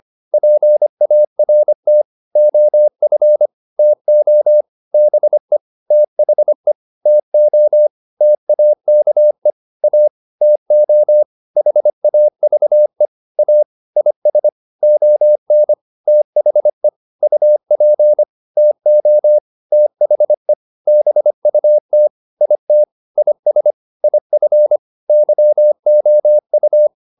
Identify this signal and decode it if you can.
Morse code, heard as a keyed tone, about 25 wpm, 600 Hz. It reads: PART OF TO BE THE TO TAKE A TO HAVE A IS ON THE UP TO THE BUT IT IS IF YOU